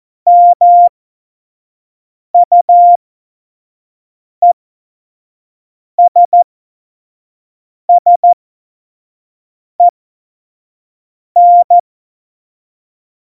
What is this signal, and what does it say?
Morse code, heard as a keyed tone, about 14 wpm, spaced as for 5 wpm, 700 Hz. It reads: MUESSEN
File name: Morsekurs / Witze / Witz 06 March 2018, 25 F70